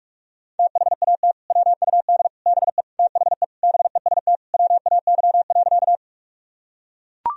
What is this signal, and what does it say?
Morse code, heard as a keyed tone, about 41 wpm, 700 Hz. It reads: THAT WUD BE THE BEST WAY.
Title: Sentences from Top 400 Words and Abbreviations 45wpm